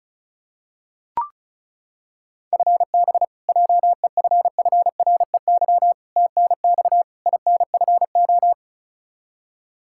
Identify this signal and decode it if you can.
Morse code, heard as a keyed tone, about 36 wpm, 700 Hz. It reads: FB JEFFREY TNX INFO
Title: K1USN SST QSO Practice 35wpm